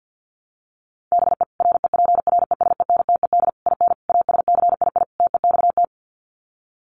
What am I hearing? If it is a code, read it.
Morse code, heard as a keyed tone, about 50 wpm, 700 Hz. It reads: BE REPRESENTED IN ASCII TEXT
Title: Sets of 5 Words 50wpm